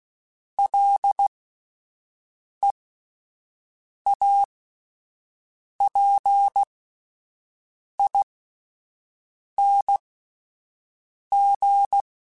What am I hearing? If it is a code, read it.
Morse code, heard as a keyed tone, about 16 wpm, spaced as for 5 wpm, 800 Hz. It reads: LEAPING